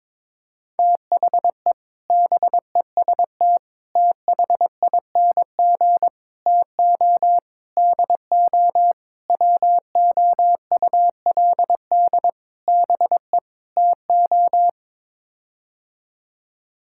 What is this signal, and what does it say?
Morse code, heard as a keyed tone, about 21 wpm, 700 Hz. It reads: THE BEST THING TO DO WOULD BE TO
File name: practice / 22wpm / Sets of 8 Words - 22wpm